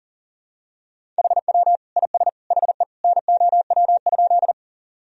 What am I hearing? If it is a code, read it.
Morse code, heard as a keyed tone, about 40 wpm, 700 Hz. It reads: HW IS HE NOW?
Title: Sentences from Top 100 Words and Abbreviations 40wpm